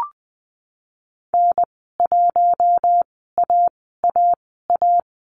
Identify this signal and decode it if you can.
Morse code, heard as a keyed tone, about 20 wpm, spaced as for 15 wpm, 700 Hz. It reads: N1AAA
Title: 154 Lesson 25g - Letter p - Call Signs 20wpm